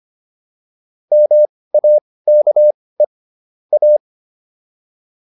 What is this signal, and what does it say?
Morse code, heard as a keyed tone, about 25 wpm, spaced as for 18 wpm, 600 Hz. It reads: MAKE A
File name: WD104 18